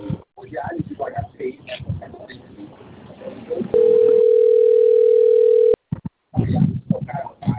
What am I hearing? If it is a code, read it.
Ringback tone (North America), 440 + 480 Hz, with speech.
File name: rg-1600-unknown- - - .27454